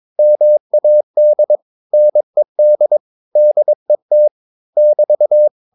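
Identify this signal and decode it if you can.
Morse code, heard as a keyed tone, about 22 wpm, 600 Hz. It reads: MAD NED DET <BT>
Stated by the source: Morse practice of short Danish words at 22 wpm, from CW med Gnister.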